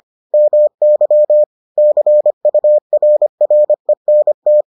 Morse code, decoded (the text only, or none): MY CURRENT